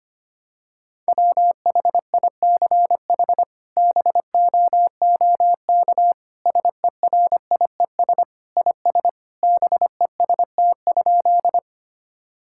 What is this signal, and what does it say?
Morse code, heard as a keyed tone, about 25 wpm, 700 Hz. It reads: WHICH BOOK SERIES IS BEST?
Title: Sentences from Top 1000 Words - Review Part 5 - 25wpm